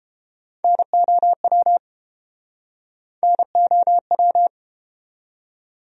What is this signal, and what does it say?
Morse code, heard as a keyed tone, about 28 wpm, 700 Hz. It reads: NOW NOW